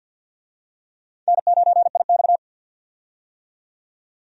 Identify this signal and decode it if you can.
Morse code, heard as a keyed tone, about 52 wpm, 700 Hz. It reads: N9IX
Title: USA Call Signs - 4 Characters - 50wpm